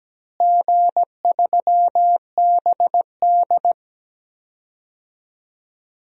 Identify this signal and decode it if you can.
Morse code, heard as a keyed tone, about 17 wpm, 700 Hz. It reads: G3BD
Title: DXCC-100 Call Signs - 4 Characters - Repeated 2x - 17wpm